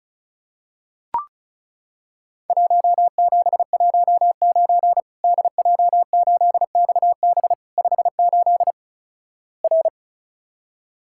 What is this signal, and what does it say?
Morse code, heard as a keyed tone, about 34 wpm, 700 Hz. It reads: 1719 DJ8XB 58 R
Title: Worked All Europe DX Contest - QTC Exchange - 35wpm